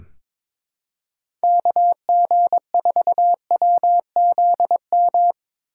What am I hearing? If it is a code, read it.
Morse code, heard as a keyed tone, about 22 wpm, 700 Hz. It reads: KG4WZM
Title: USA Call Signs - 6 Characters - 3x Character Spacing - 22wpm